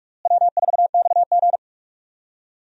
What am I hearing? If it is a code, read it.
Morse code, heard as a keyed tone, about 46 wpm, 700 Hz. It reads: W4XG
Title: USA Call Signs - 4 Characters - Repeated 2x - 45wpm